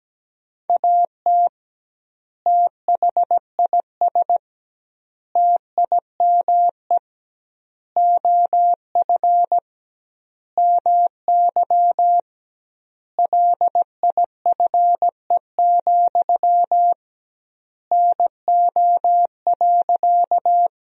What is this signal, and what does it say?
Morse code, heard as a keyed tone, about 17 wpm, 700 Hz. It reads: AT THIS TIME OF MY LIFE, NO.
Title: 2x Word-Spacing - Sentences from Top 300 Words - 17wpm